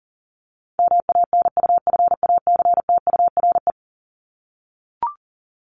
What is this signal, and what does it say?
Morse code, heard as a keyed tone, about 40 wpm, 700 Hz. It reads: MANUFACTURE
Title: ING Suffix - Top 100 Words with Base Words 40wpm